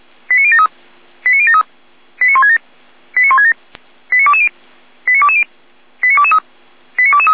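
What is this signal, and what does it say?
Five-tone selective call (ZVEI), heard as quick runs of five tones, tones of about 70 ms. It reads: 89073 89073 89167 89167 89220 89220 89223 89223
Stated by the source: ZVEI five-tone selective call received by SDR.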